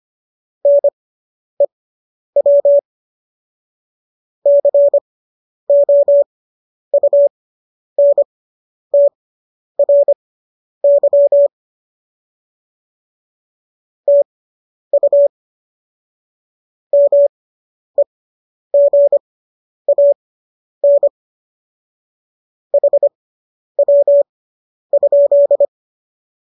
Morse code, heard as a keyed tone, about 25 wpm, spaced as for 10 wpm, 570 Hz. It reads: NEW COUNTRY TU MEGAN HW?